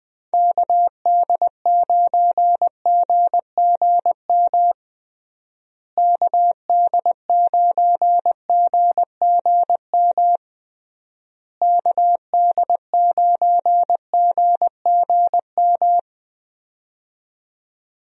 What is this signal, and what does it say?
Morse code, heard as a keyed tone, about 20 wpm, 700 Hz. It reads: KD9GGM KD9GGM KD9GGM
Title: USA Call Signs - 6 Characters - Repeated 3x - 20wpm